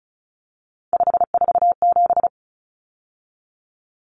Morse code, heard as a keyed tone, about 34 wpm, 700 Hz. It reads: 547
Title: Numbers - 3-Digit 35wpm